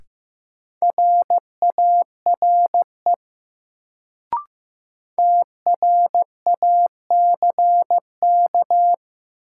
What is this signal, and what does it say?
Morse code, heard as a keyed tone, about 16 wpm, 700 Hz. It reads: RARE TRACK
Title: Sentences from Top 1100 Words - New Word Drill - 15wpm